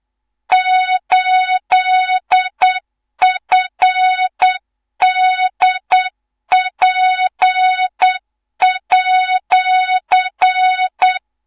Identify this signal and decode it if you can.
Morse code, heard as a keyed tone, about 8 wpm, 3700 Hz, with noise. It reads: 8FDP@
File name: Morze-HA2MN / hang / 18 IRASJELEK2